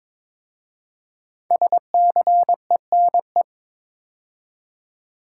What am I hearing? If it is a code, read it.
Morse code, heard as a keyed tone, about 23 wpm, 700 Hz. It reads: SCENE
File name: 5 Letter Words - First 3 Letters Given - 22wpm